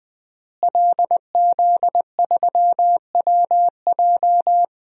Morse code, heard as a keyed tone, about 20 wpm, 700 Hz. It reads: LZ3WJ
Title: DXCC-100 Call Signs - 5 Characters - Repeated 2x - 20wpm